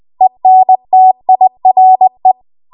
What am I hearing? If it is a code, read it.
Morse code, heard as a keyed tone, about 20 wpm, 750 Hz. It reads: ENTIRE
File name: Archive / 20 WPM / 20WPM